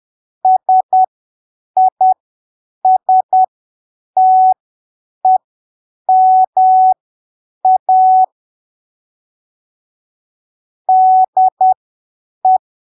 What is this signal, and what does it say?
Morse code, heard as a keyed tone, about 10 wpm, spaced as for 7 wpm, 750 Hz. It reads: SISTEMA DE
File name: arquivos / texto / texto 22 10ppm